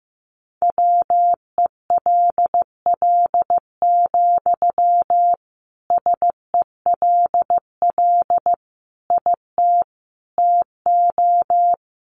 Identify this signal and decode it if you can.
Morse code, heard as a keyed tone, about 15 wpm, 700 Hz. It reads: WELL, SELL IT TO
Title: Sentences from Top 1100 Words 15wpm